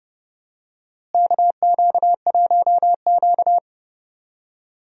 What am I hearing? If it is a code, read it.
Morse code, heard as a keyed tone, about 30 wpm, 700 Hz. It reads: KQ1Q
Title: USA Call Signs - 4 Characters - Repeated 3x - 30wpm